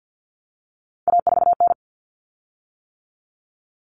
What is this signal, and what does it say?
Morse code, heard as a keyed tone, about 50 wpm, 700 Hz. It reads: A4N
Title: Common Cut Numbers - 3-Digit 50wpm